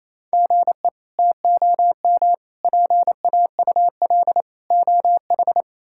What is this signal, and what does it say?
Morse code, heard as a keyed tone, about 28 wpm, 700 Hz. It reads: GE TOM PAUL OH
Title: K1USN SST QSO Practice - 28wpm